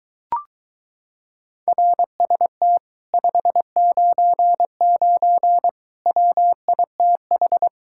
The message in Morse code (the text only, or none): RST 599 WITH